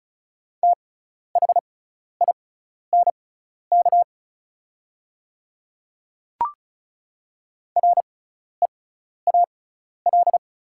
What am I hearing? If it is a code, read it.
Morse code, heard as a keyed tone, about 36 wpm, spaced as for 12 wpm, 700 Hz. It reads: THINK REAL
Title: 6x Character Spacing - Top 300 Words - 35wpm